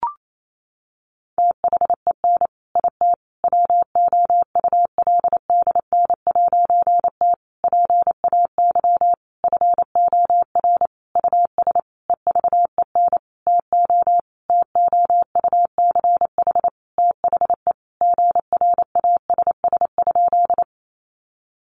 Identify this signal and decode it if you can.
Morse code, heard as a keyed tone, about 28 wpm, 700 Hz. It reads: THEN IT WOULDN'T PAY FOR US EVEN TO TOUCH THE GRASS?
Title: A Sound Of Thunder 28wpm